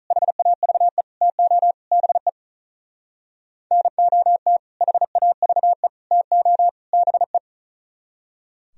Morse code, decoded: HAVE TO BE NOT HAVE TO BE

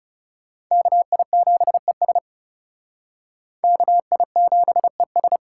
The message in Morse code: KI7ES KI7ES

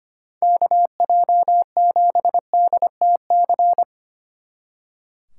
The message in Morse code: KJ7DTC

